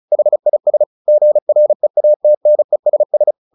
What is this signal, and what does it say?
Morse code, heard as a keyed tone, about 34 wpm, 600 Hz. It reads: HIS GREATNESS
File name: the old man and the sea / the old man and the sea 1 35wpm